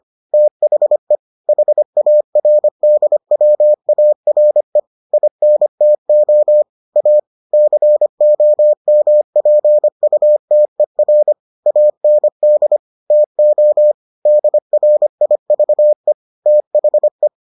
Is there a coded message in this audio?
Yes: Morse code from 0.3 to 17.3 s, heard as a keyed tone, about 25 wpm, 600 Hz. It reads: THE HARDWARE INTO A COMPUTER AND TO DRIVE THE